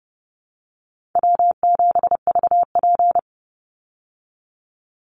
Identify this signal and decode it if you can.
Morse code, heard as a keyed tone, about 30 wpm, 700 Hz. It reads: W7VP
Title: Easy Call Signs Repeated 3x - 30wpm